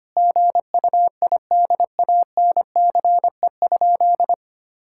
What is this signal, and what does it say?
Morse code, heard as a keyed tone, about 25 wpm, 700 Hz. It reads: GUIDANCE?
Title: Sentences from Top 1600 Words 25wpm